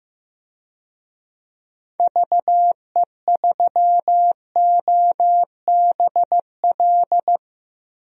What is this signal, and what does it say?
Morse code, heard as a keyed tone, about 15 wpm, 700 Hz. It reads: VE3OBL
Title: DXCC-100 Call Signs - 6 Characters - Repeated 2x - 15wpm